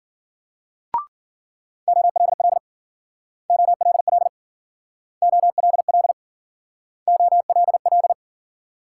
Morse code, heard as a keyed tone, about 49 wpm, 700 Hz. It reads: OLL OLL OLL OLL
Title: Speed-Racing Multi-letter Phonemes - 40wpm